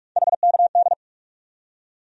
Morse code, heard as a keyed tone, about 42 wpm, 700 Hz. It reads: HKD
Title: USA Call Sign Postfixes - 3 Letters 45wpm